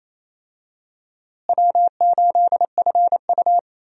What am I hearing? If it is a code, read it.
Morse code, heard as a keyed tone, about 28 wpm, 700 Hz. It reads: W8FU
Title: USA Call Signs - 4 Characters - Repeated 2x - 28wpm